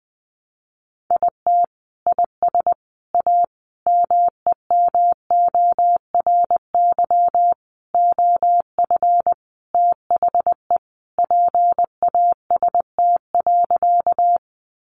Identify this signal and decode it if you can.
Morse code, heard as a keyed tone, about 20 wpm, 700 Hz. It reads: IT IS A MEMORY OF THE PAST.
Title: Sentences from Top 1300 Words 20wpm